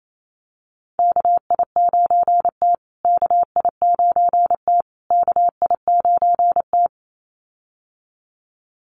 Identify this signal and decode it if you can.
Morse code, heard as a keyed tone, about 28 wpm, 700 Hz. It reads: KI9T KI9T KI9T
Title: USA Call Signs Repeated 3x 28wpm